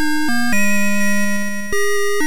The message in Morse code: none